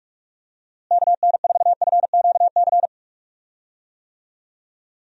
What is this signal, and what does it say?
Morse code, heard as a keyed tone, about 41 wpm, 700 Hz. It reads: KN4FQC
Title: USA Call Signs - 6 Characters - Repeated 2x - 45wpm